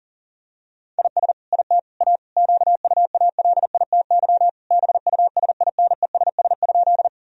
Morse code, heard as a keyed tone, about 40 wpm, 700 Hz. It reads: IS IT A QUALITY BUSINESS?
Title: Sentences from Top 1000 Words - Review Part 4 - 40wpm